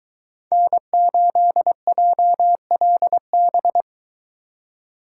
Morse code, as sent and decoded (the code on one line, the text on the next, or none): -. ---.. .--- .-.. -...
N8JLB